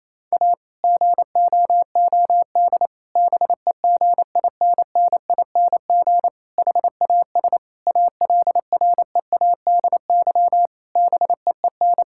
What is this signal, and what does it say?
Morse code, heard as a keyed tone, about 28 wpm, 700 Hz. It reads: A GOOD BEGINNING HAS ALREADY BEEN
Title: Sentences from Top 1000 Words - Review Part 5 - 28wpm